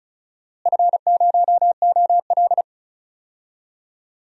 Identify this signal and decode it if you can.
Morse code, heard as a keyed tone, about 36 wpm, 700 Hz. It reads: F0OL
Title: DXCC-100 Call Signs - 4 Characters - Repeated 2x - 35wpm